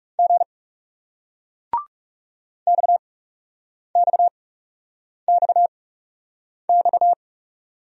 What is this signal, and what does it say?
Morse code, heard as a keyed tone, about 40 wpm, spaced as for 8 wpm, 700 Hz. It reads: G XXXX